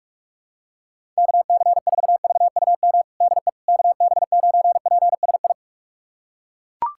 Morse code, heard as a keyed tone, about 41 wpm, 700 Hz. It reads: KC4VUM DE KB9PSI
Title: USA Callsign DE Callsign - 6 Characters - 45wpm